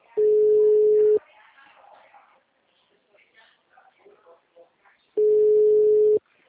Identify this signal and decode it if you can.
Ringback tone (Europe, 425 Hz), with speech.